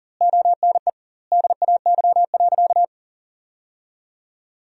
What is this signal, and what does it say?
Morse code, heard as a keyed tone, about 40 wpm, 700 Hz. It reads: ONE DAY.
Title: 2x Word-Spacing - Sentences from Top 300 Words - 40wpm